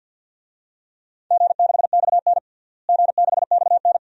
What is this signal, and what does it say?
Morse code, heard as a keyed tone, about 53 wpm, 700 Hz. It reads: G6XN G6XN
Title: DXCC-100 Call Signs - 4 Characters - Repeated 2x - 50wpm